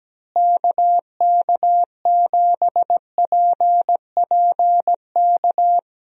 Morse code, heard as a keyed tone, about 17 wpm, 700 Hz. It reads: KK7PPK
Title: USA Call Signs - 6 Characters - Repeated 2x - 17wpm